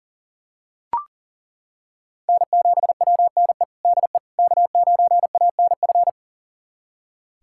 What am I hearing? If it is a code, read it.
Morse code, heard as a keyed tone, about 40 wpm, 700 Hz. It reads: N7WNE DE K9ANF
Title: USA Callsign DE Callsign - 5 Characters - 40wpm